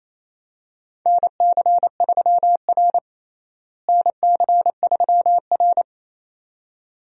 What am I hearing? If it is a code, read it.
Morse code, heard as a keyed tone, about 28 wpm, 700 Hz. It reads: NC3R NC3R
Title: USA Call Signs - 4 Characters - Repeated 2x - 28wpm